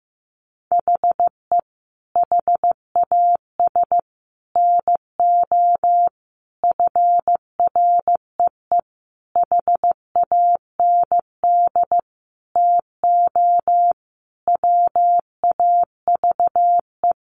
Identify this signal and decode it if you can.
Morse code, heard as a keyed tone, about 14 wpm, 700 Hz. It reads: HE HAS NO FREE HAND TO WAVE
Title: Sentences from Top 1400 Words 15wpm